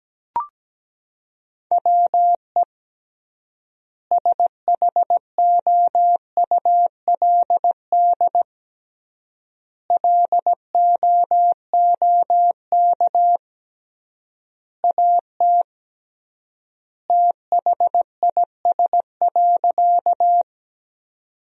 Morse code, heard as a keyed tone, about 17 wpm, 700 Hz. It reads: WE SHOULD LOOK AT THIS.
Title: 3x Word-Spacing - Sentences from Top 300 Words - 17wpm